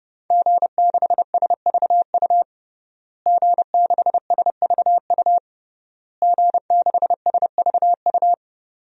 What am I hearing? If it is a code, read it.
Morse code, heard as a keyed tone, about 30 wpm, 700 Hz. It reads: G6SVU G6SVU G6SVU
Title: DXCC-100 Call Signs - 5 Characters - Repeated 3x - 30wpm